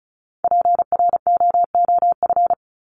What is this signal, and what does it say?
Morse code, heard as a keyed tone, about 34 wpm, 700 Hz. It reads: PROOF
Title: Top 1501-1750 Words 35wpm